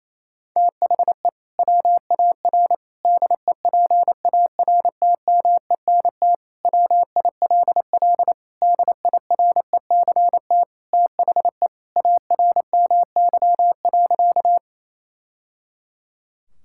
Morse code, heard as a keyed tone, about 28 wpm, 700 Hz. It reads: THE WAR DEPARTMENT WILL DIRECT THE ARMY.
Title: Sentences from Top 1000 Words - Review Part 5 - 28wpm